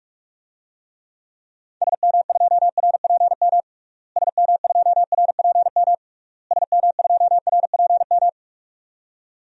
Morse code, heard as a keyed tone, about 46 wpm, 700 Hz. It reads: SM2RPM SM2RPM SM2RPM
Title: DXCC-100 Call Signs - 6 Characters - Repeated 3x - 45wpm